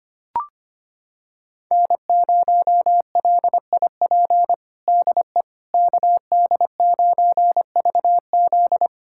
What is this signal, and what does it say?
Morse code, heard as a keyed tone, about 25 wpm, 700 Hz. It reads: N0LIP DE KD9VZ
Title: USA Callsign DE Callsign - 5 Characters - 25wpm